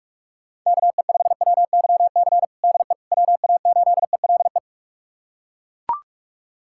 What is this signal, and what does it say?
Morse code, heard as a keyed tone, about 41 wpm, 700 Hz. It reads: KE5WYC DE WA8ELE
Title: USA Callsign DE Callsign - 6 Characters - 45wpm